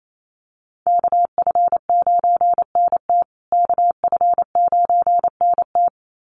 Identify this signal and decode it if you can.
Morse code, heard as a keyed tone, about 28 wpm, 700 Hz. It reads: KF9NT KF9NT
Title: USA Call Signs Repeated 2x 28wpm